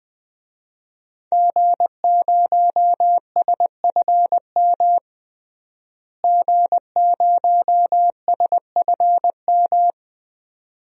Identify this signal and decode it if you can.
Morse code, heard as a keyed tone, about 20 wpm, 700 Hz. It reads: G0SFM G0SFM
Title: DXCC-100 Call Signs - 5 Characters - Repeated 2x - 20wpm